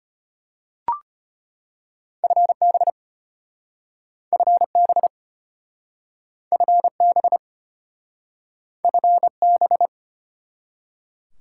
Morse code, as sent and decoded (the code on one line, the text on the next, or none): ..-. -... / ..-. -... / ..-. -... / ..-. -...
FB FB FB FB